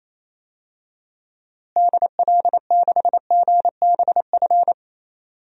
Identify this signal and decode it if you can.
Morse code, heard as a keyed tone, about 28 wpm, 700 Hz. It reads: DL6GBF
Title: DXCC-100 Call Signs - 6 Characters - Repeated 2x - 28wpm